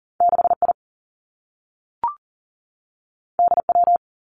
Morse code, heard as a keyed tone, about 39 wpm, 700 Hz. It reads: 6 I D W
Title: Two Challenging Letter-Number Combinations 40wpm